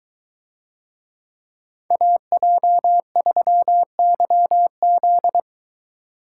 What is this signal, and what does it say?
Morse code, heard as a keyed tone, about 22 wpm, 700 Hz. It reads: AJ3YZ